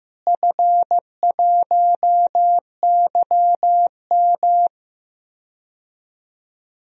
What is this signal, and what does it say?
Morse code, heard as a keyed tone, about 15 wpm, 700 Hz. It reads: F1YM
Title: DXCC-100 Call Signs - 4 Characters - Repeated 3x - 15wpm